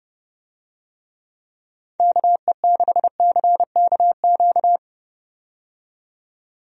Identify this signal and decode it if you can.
Morse code, heard as a keyed tone, about 30 wpm, 700 Hz. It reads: KE6CKQ
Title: USA Call Signs - 6 Characters - 30wpm